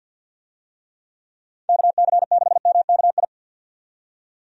Morse code, heard as a keyed tone, about 51 wpm, 700 Hz. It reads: KC6MKI
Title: USA Call Signs - 6 Characters - 50wpm